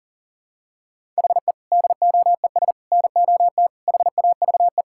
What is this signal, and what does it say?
Morse code, heard as a keyed tone, about 40 wpm, 700 Hz. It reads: HE DOES NOT HAVE
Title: Sentences from Top 1000 Words - Review Part 5 - 40wpm